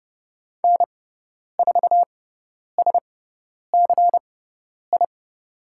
Morse code, heard as a keyed tone, about 30 wpm, spaced as for 10 wpm, 700 Hz. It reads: N4SCI
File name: Lesson 17g - Letter u - Call Signs 30wpm